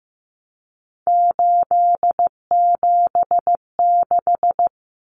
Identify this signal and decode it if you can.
Morse code, heard as a keyed tone, about 15 wpm, 700 Hz. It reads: 876
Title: Numbers - 3-Digit 15wpm